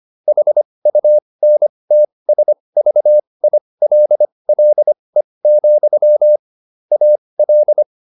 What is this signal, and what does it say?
Morse code, heard as a keyed tone, about 25 wpm, spaced as for 20 wpm, 600 Hz. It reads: HUNTSVILLE, AL